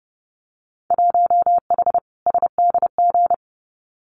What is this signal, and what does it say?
Morse code, heard as a keyed tone, about 30 wpm, 700 Hz. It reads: 1H SDG
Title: Winter Field Day Exchanges 30wpm